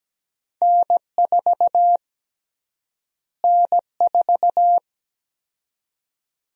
Morse code, heard as a keyed tone, about 17 wpm, 700 Hz. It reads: N4 N4
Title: USA Call Sign Prefixes - Repeated 2x - 17wpm